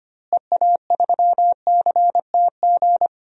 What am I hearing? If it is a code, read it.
Morse code, heard as a keyed tone, about 25 wpm, 700 Hz. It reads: EA3CTG
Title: DXCC-100 Call Signs - 6 Characters - Repeated 3x - 25wpm